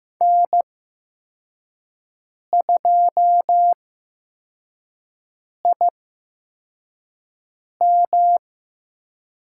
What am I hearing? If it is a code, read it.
Morse code, heard as a keyed tone, about 15 wpm, spaced as for 4 wpm, 700 Hz. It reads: N2IM
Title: USA Call Signs - 4 Characters - 8x Character Spacing - 15wpm